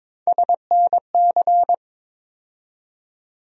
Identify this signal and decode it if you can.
Morse code, heard as a keyed tone, about 22 wpm, 700 Hz. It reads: SNC
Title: USA Call Sign Postfixes - 3 Letters 22wpm